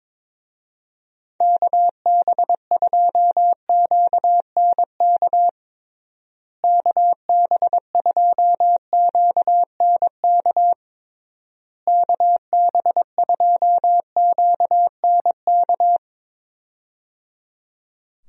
Morse code, heard as a keyed tone, about 22 wpm, 700 Hz. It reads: KB2QNK KB2QNK KB2QNK
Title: USA Call Signs - 6 Characters - Repeated 3x - 22wpm